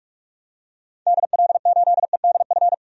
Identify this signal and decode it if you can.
Morse code, heard as a keyed tone, about 42 wpm, 700 Hz. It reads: DL8EDF
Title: DXCC-100 Call Signs - 6 Characters - Repeated 3x - 45wpm